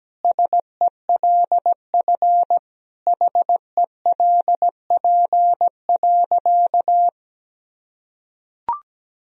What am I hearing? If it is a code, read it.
Morse code, heard as a keyed tone, about 17 wpm, 700 Hz. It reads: SELF HELP.
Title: Sentences from Top 1000 Words - Review Part 1 - 17wpm